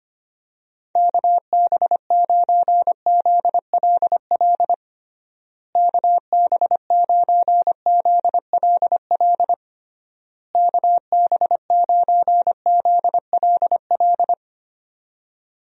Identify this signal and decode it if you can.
Morse code, heard as a keyed tone, about 25 wpm, 700 Hz. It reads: KB9ZLL KB9ZLL KB9ZLL
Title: USA Call Signs - 6 Characters - Repeated 3x - 25wpm